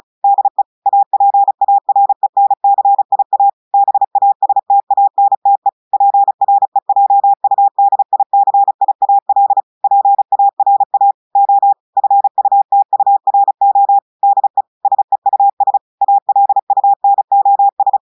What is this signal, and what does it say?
Morse code, heard as a keyed tone, about 35 wpm, 800 Hz. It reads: DE APARENCIA BASTANTE PREJUDICIAL PARA O FUTURO DE SEUS ALUNOS
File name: audios / portugais / portugais texte en clair 1 vitesse 35 mots minutes